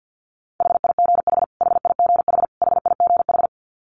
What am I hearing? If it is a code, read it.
Morse code, heard as a keyed tone, about 48 wpm, 700 Hz. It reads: HIGH HIGH HIGH
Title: Top 200 Words Repeated 50wpm